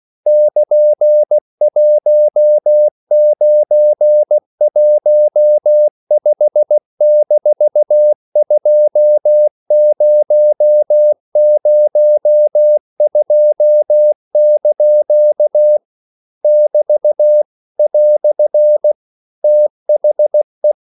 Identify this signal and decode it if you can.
Morse code, heard as a keyed tone, about 16 wpm, 600 Hz. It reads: <KN>1915-2002) <BT> "THE
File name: CW / Practice / EnglishQuotations0416